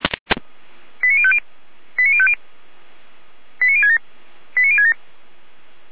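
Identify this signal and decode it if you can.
Five-tone selective call (ZVEI), heard as quick runs of five tones, tones of about 70 ms. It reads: 89050 89050 89067 89067
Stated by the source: ZVEI five-tone selective call received by SDR.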